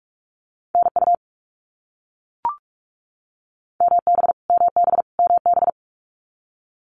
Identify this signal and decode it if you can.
Morse code, heard as a keyed tone, about 44 wpm, 700 Hz. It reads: NU MB MB MB
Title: Canadian Provinces and Territories - Repeated 3x - 45wpm